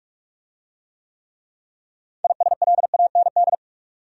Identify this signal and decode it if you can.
Morse code, heard as a keyed tone, about 42 wpm, 700 Hz. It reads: ISLAND